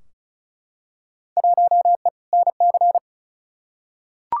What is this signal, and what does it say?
Morse code, heard as a keyed tone, about 36 wpm, 700 Hz. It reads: 1E NC
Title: Field Day Exchanges - 35wpm